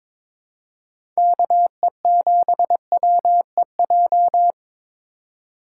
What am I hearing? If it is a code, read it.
Morse code, heard as a keyed tone, about 22 wpm, 700 Hz. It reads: KE7WEJ
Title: USA Call Signs - 6 Characters - 2x Character Spacing - 22wpm